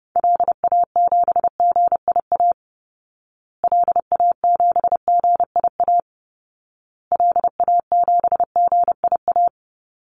Morse code, heard as a keyed tone, about 30 wpm, 700 Hz. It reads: LA7GIA LA7GIA LA7GIA
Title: Call Signs Repeated 2x - 30wpm